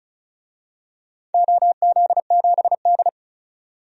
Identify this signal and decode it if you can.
Morse code, heard as a keyed tone, about 36 wpm, 700 Hz. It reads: OZ7D
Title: DXCC-100 Call Signs - 4 Characters - 35wpm